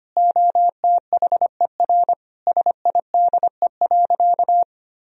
Morse code, heard as a keyed tone, about 25 wpm, 700 Hz. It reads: OTHER SIDE.
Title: Sentences from Top 1000 Words - Review Part 1 - 25wpm